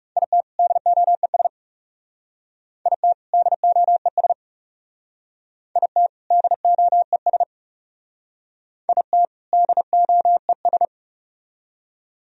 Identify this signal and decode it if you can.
Morse code, heard as a keyed tone, about 36 wpm, 700 Hz. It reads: IT DOES IT DOES IT DOES IT DOES
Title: Speed-Racing - Sets of 2 Words 30wpm